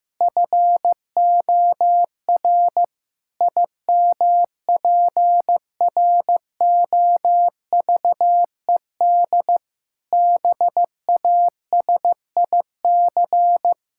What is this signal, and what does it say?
Morse code, heard as a keyed tone, about 15 wpm, 700 Hz. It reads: FOR IMPROVED BASIC
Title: Pangram Challenge v2 - 15wpm